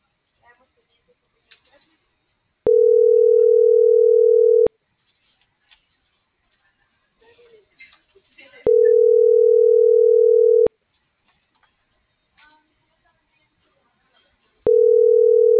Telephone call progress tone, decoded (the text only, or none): ringback tone